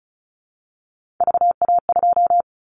Morse code, heard as a keyed tone, about 36 wpm, 700 Hz. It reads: VA2
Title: Callsign Prefixes for Top 25 DXCC - 35wpm